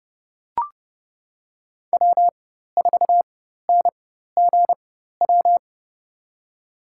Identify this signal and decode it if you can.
Morse code, heard as a keyed tone, about 31 wpm, spaced as for 14 wpm, 690 Hz. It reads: W4NGW
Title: USA Call Signs - 4x Character Spacing - 30wpm